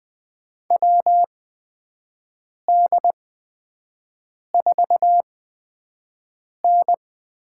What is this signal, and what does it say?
Morse code, heard as a keyed tone, about 21 wpm, spaced as for 5 wpm, 700 Hz. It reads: WD4N
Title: USA Call Signs - 6 Characters - 8x Character Spacing - 20wpm